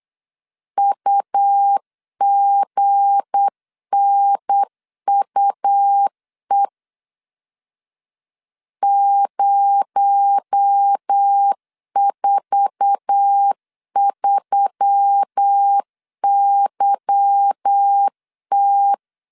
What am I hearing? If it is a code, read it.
Morse code, heard as a keyed tone, about 8 wpm, 800 Hz. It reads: UGNUE 043YT
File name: LEZIONI / 9 2867 8wpm 3440